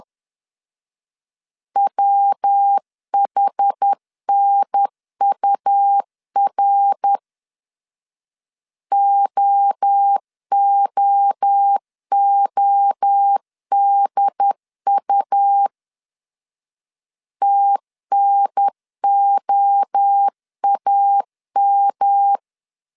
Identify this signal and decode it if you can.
Morse code, heard as a keyed tone, about 10 wpm, 800 Hz. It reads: WHNUR OOODU TNOAM
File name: CW / DET / 1 3 1500C S B 10wpm 3335